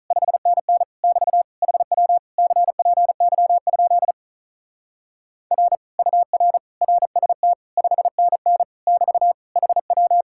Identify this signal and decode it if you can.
Morse code, heard as a keyed tone, about 44 wpm, 700 Hz. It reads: 5NN <BT> HW CPY? R UR RST 5NN <BT> HW